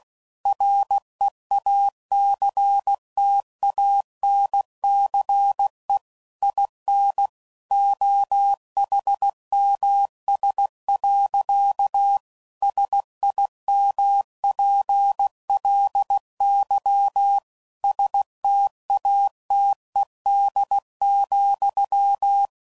Morse code, heard as a keyed tone, about 16 wpm, 800 Hz. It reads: REACTANCE IN OHMS. SIMPLY STATED,
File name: CW Code Groups @15WPM / Technical Subjects / XC